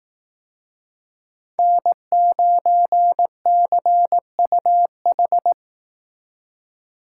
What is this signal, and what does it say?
Morse code, heard as a keyed tone, about 18 wpm, 700 Hz. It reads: N9CUH
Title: Speed Racing - USA 1x3 Call Signs 15wpm